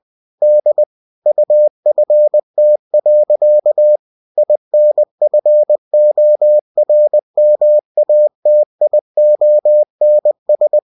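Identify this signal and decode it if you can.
Morse code, heard as a keyed tone, about 20 wpm, 600 Hz. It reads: D UFT. INFORMATIONS
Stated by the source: Bulletin sent in CW.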